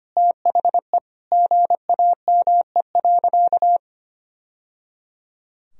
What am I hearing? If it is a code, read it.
Morse code, heard as a keyed tone, about 25 wpm, 700 Hz. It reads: THE GAME.
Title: Sentences from Top 1000 Words - Review Part 3 - 25wpm